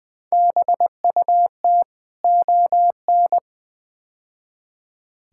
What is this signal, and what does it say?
Morse code, heard as a keyed tone, about 20 wpm, 700 Hz. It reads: BUT ON
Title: Build A Phrase - 5-Word Phrases - 20wpm